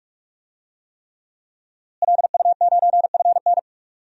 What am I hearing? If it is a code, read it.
Morse code, heard as a keyed tone, about 46 wpm, 700 Hz. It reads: LU9FN
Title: DXCC-100 Call Signs - 5 Characters - Repeated 3x - 45wpm